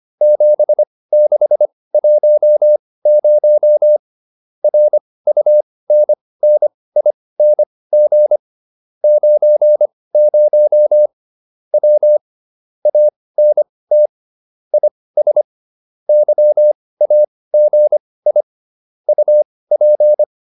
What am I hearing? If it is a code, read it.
Morse code, heard as a keyed tone, about 25 wpm, spaced as for 18 wpm, 600 Hz. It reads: 7610 RUNNING 90 W ANT IS YAGI UP